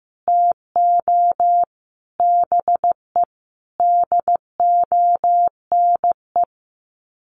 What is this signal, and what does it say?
Morse code, heard as a keyed tone, about 15 wpm, 700 Hz. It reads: TO BE DONE
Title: Sets of 5 Words Encore 15wpm